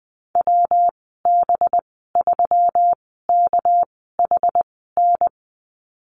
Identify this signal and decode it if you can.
Morse code, heard as a keyed tone, about 19 wpm, 700 Hz. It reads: W B 3 K H N
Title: 207 Lesson 33g - Letter j - Call Signs 20wpm